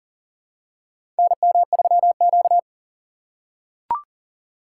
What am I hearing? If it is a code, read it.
Morse code, heard as a keyed tone, about 40 wpm, 700 Hz. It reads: NM3Q
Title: USA Call Signs - 4 Characters - 6x Character Spacing - 40wpm